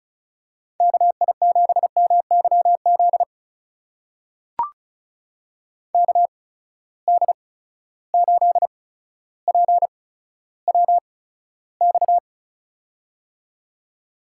Morse code, heard as a keyed tone, about 36 wpm, spaced as for 15 wpm, 700 Hz. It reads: KI7MYZ KD8PWX
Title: USA Call Signs - 8x Character Spacing - 35wpm